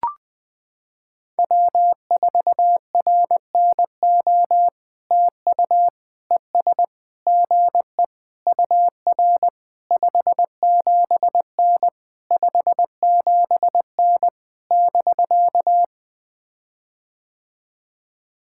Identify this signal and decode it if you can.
Morse code, heard as a keyed tone, about 20 wpm, 690 Hz. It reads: W4RNO TU ES GE UR 57N 57N <BK>
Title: POTA QSO Practice - 20wpm